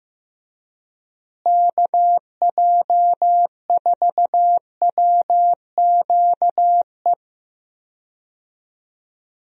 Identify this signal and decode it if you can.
Morse code, heard as a keyed tone, about 15 wpm, 700 Hz. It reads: KJ4WQE
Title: USA Call Signs - 6 Characters - Repeated 2x - 15wpm